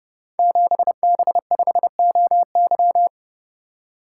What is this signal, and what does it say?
Morse code, heard as a keyed tone, about 28 wpm, 700 Hz. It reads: 7B5OY
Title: Code Groups - 5 Letter:Numbers - 30wpm